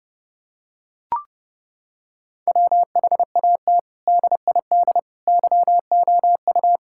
Morse code, heard as a keyed tone, about 32 wpm, 700 Hz. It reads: WHAT DID YOU
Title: Sentences from Top 1000 Words - Review Part 5 - 30wpm